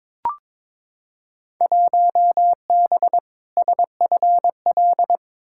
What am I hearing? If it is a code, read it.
Morse code, heard as a keyed tone, about 22 wpm, 700 Hz. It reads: 1B SFL
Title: Field Day Exchanges Encore - 22wpm